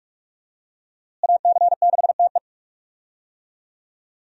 Morse code, heard as a keyed tone, about 47 wpm, 700 Hz. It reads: AC6TE
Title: USA Call Signs - 5 Characters - Repeated 2x - 45wpm